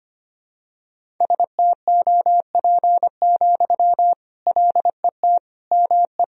Morse code, heard as a keyed tone, about 25 wpm, 700 Hz. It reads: STOP, LET ME
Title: Sentences from Top 300 Words and Abbreviations 25wpm